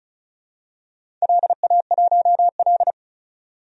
Morse code, heard as a keyed tone, about 36 wpm, 690 Hz. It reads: LA1L
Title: DXCC-100 Call Signs - 4 Characters - 35wpm